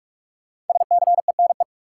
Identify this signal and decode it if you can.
Morse code, heard as a keyed tone, about 46 wpm, 700 Hz. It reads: SCENE
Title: Sentences from Top 1100 Words - New Word Drill - 45wpm